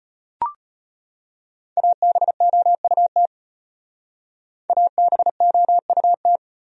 Morse code, heard as a keyed tone, about 34 wpm, 700 Hz. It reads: ABOUT ABOUT